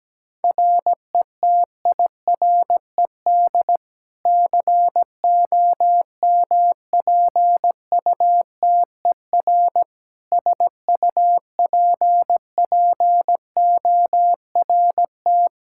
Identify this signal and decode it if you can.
Morse code, heard as a keyed tone, about 17 wpm, 700 Hz. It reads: RETIRED COMPUTER SUPPORT
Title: Practice QSO - Exchange 3 - More Info - 17wpm